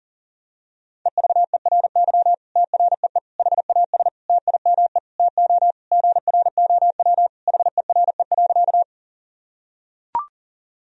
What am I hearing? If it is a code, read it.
Morse code, heard as a keyed tone, about 37 wpm, 700 Hz. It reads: EVERY TREE HAS TIME TO GROW HERE.
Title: Sentences from Top 1000 Words - Review Part 1 - 40wpm